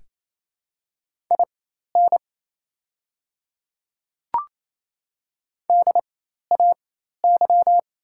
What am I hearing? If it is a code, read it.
Morse code, heard as a keyed tone, about 28 wpm, spaced as for 13 wpm, 700 Hz. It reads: IN DAY
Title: 4x Character Spacing - Top 100 Words - 28wpm